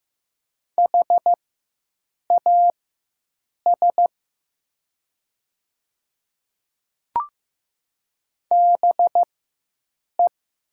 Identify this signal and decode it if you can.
Morse code, heard as a keyed tone, about 15 wpm, spaced as for 7 wpm, 700 Hz. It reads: HAS BE
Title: 4x Character Spacing - Top 100 Words - 15wpm